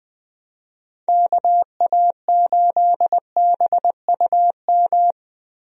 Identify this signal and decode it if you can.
Morse code, heard as a keyed tone, about 20 wpm, 700 Hz. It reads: KA8BUM
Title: USA Call Signs - 6 Characters - Repeated 3x - 20wpm